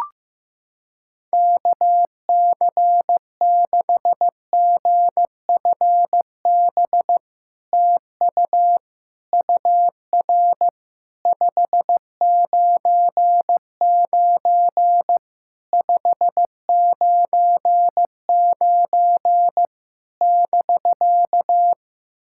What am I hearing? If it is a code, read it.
Morse code, heard as a keyed tone, about 16 wpm, 700 Hz. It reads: KC6GFB TU UR 599 599 <BK>
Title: SOTA QSO Practice 15wpm